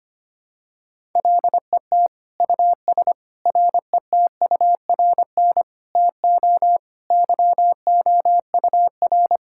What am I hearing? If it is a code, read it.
Morse code, heard as a keyed tone, about 25 wpm, 700 Hz. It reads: LET US RETURN TO YOUR
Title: Sentences from Top 1000 Words - Review Part 6 - 25wpm